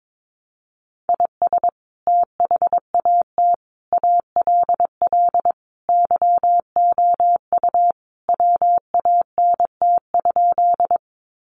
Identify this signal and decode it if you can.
Morse code, heard as a keyed tone, about 22 wpm, 700 Hz. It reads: IS THAT ALL YOU WANT?
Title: Sentences from Top 200 Words 22wpm